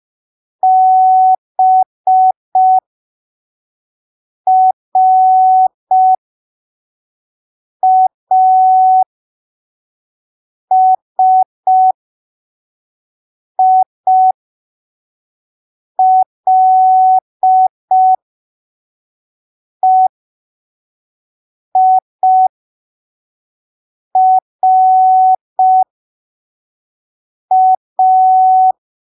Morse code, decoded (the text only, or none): BRASILEIRA